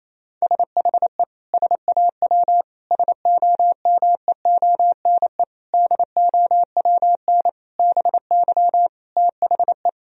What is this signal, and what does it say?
Morse code, heard as a keyed tone, about 28 wpm, 700 Hz. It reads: SHE SAW SOMEONE DOWN BY THE
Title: Sentences from Top 1000 Words - Review Part 4 - 28wpm